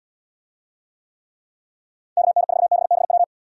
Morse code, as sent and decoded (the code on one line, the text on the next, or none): -.- -. ....- -... -... .--
KN4BBW